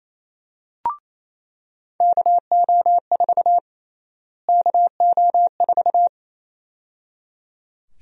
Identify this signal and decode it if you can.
Morse code, heard as a keyed tone, about 27 wpm, 700 Hz. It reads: KO4 KO4
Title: USA Call Sign Prefixes - Repeated 2x - 28wpm